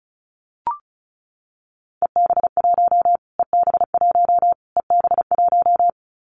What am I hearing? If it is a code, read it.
Morse code, heard as a keyed tone, about 34 wpm, 700 Hz. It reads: EB1 EB1 EB1
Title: Callsign Prefixes for Top 25 DXCC - Repeated 3x - 35wpm